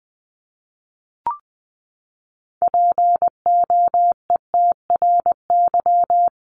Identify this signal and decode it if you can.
Morse code, heard as a keyed tone, about 20 wpm, 700 Hz. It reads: POETRY
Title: Top 1751-2000 Words 20wpm